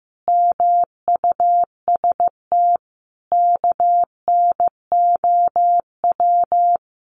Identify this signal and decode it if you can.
Morse code, heard as a keyed tone, about 15 wpm, 700 Hz. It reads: MUST KNOW
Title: Sentences from Top 1400 Words 15wpm